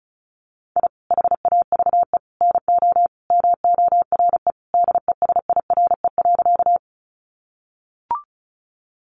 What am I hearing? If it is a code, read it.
Morse code, heard as a keyed tone, about 34 wpm, 700 Hz. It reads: I HAVE NO MORE DESIRE.
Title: Sentences from Top 1100 Words 35wpm